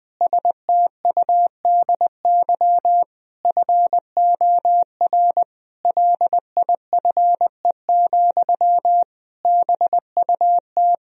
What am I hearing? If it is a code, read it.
Morse code, heard as a keyed tone, about 20 wpm, 700 Hz. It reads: STUDY FOR LIFE, BUT
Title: Sentences from Top 1000 Words - Review Part 1 - 20wpm